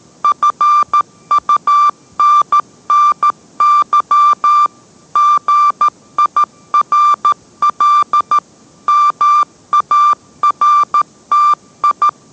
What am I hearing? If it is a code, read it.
Morse code, heard as a keyed tone, about 13 wpm, 1180 Hz. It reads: FUNNY GIRL MARTI